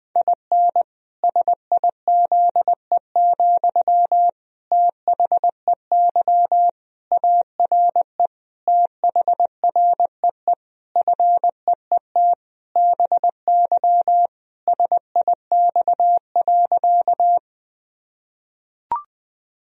Morse code, heard as a keyed tone, about 20 wpm, 700 Hz. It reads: IN SIZE, THEY ARE THREE FEET BY SIX.
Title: Sentences from Top 1000 Words - Review Part 2 - 20wpm